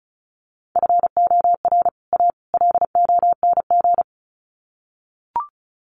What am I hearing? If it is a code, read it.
Morse code, heard as a keyed tone, about 34 wpm, 700 Hz. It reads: FOR A LONG
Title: Sets of 3 Words 35wpm